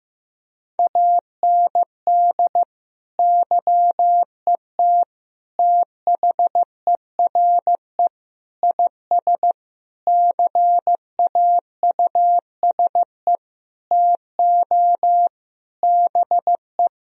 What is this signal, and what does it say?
Morse code, heard as a keyed tone, about 15 wpm, 700 Hz. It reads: AND YET THERE IS CAUSE TO BE
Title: Sentences from Top 1000 Words - Review Part 1 - 15wpm